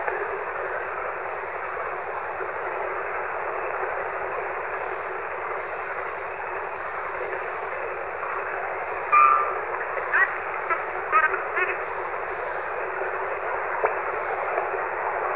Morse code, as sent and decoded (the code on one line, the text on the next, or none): none